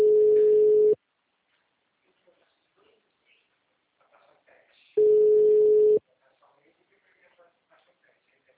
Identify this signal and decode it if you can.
Ringback tone (Europe, 425 Hz), with speech.
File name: out- -2002- - - .517